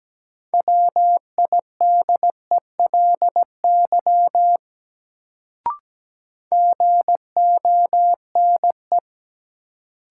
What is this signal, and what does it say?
Morse code, heard as a keyed tone, about 17 wpm, 700 Hz. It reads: WIDELY GONE
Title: Sentences from Top 1200 Words - New Word Drill - 17wpm